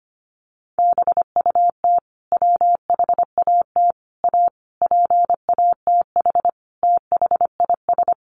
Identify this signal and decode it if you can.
Morse code, heard as a keyed tone, about 25 wpm, 700 Hz. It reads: BUT WHAT A PATH THIS
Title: Sentences from Top 1400 Words 25wpm